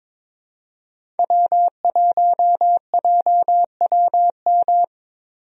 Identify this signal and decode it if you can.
Morse code, heard as a keyed tone, about 22 wpm, 700 Hz. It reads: W1JWM
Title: USA Call Signs - 5 Characters - 22wpm